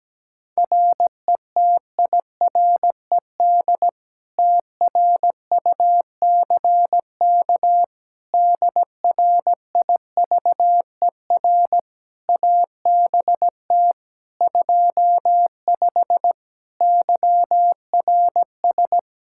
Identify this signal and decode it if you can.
Morse code, heard as a keyed tone, about 17 wpm, 700 Hz. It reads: RETIRED TRUCK DRIVER ABT 25 YRS
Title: Practice QSO - Exchange 3 - More Info - 17wpm